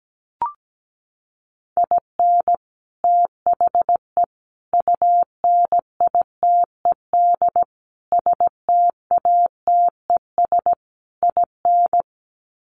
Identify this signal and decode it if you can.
Morse code, heard as a keyed tone, about 17 wpm, 700 Hz. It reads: IN THE UNITED STATES IN
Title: Sets of 5 Words 17wpm